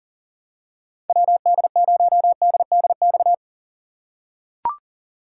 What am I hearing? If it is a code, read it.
Morse code, heard as a keyed tone, about 38 wpm, 700 Hz. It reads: WD0DDX
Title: USA Call Signs - 6 Characters - 6x Character Spacing - 40wpm